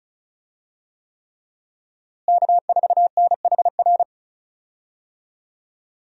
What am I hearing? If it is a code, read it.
Morse code, heard as a keyed tone, about 34 wpm, 700 Hz. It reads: K4NHR
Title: USA Call Signs - 5 Characters - Repeated 3x - 35wpm